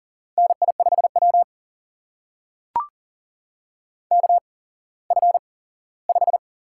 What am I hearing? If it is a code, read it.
Morse code, heard as a keyed tone, about 40 wpm, spaced as for 19 wpm, 700 Hz. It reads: NI5W KF5
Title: USA Call Signs - 8x Character Spacing - 40wpm